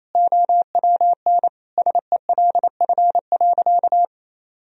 Morse code, heard as a keyed tone, about 28 wpm, 700 Hz. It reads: OWN SELF.
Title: Sentences from Top 1000 Words - Review Part 1 - 28wpm